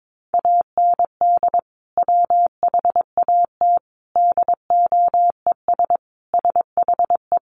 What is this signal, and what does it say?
Morse code, heard as a keyed tone, about 22 wpm, 700 Hz. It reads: AND WHAT DOES SHE
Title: Sentences from Top 400 Words 22wpm